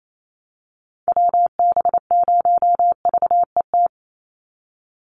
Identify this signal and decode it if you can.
Morse code, heard as a keyed tone, about 28 wpm, 700 Hz. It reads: WB0VET
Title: USA Call Signs 28wpm